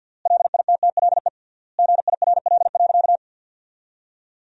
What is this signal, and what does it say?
Morse code, heard as a keyed tone, about 50 wpm, 700 Hz. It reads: LITTLE GIRL.
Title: 3x Word-Spacing - Sentences from Top 400 Words - 50wpm